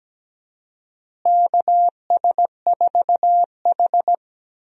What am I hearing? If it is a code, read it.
Morse code, heard as a keyed tone, about 17 wpm, 700 Hz. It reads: KS4H
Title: USA Call Signs - 4 Characters - Repeated 3x - 17wpm